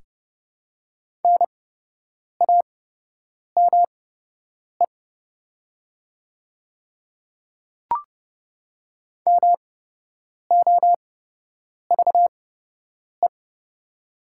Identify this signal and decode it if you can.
Morse code, heard as a keyed tone, about 30 wpm, spaced as for 8 wpm, 700 Hz. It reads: NAME MOVE
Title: 8x Character Spacing - Top 300 Words - 30wpm